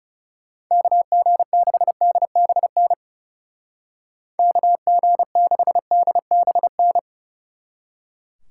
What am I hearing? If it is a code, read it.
Morse code, heard as a keyed tone, about 37 wpm, 700 Hz. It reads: KG6DBN KG6DBN
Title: Speed Racing - USA Call Signs 30wpm